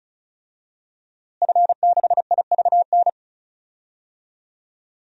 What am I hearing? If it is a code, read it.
Morse code, heard as a keyed tone, about 34 wpm, 700 Hz. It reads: F6IVN
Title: DXCC-100 Call Signs - 5 Characters - Repeated 3x - 35wpm